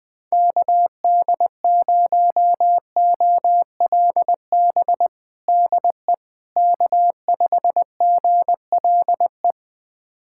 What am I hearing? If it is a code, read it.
Morse code, heard as a keyed tone, about 20 wpm, 700 Hz. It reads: KD0OLB DE K5GLE
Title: USA Callsign DE Callsign 20wpm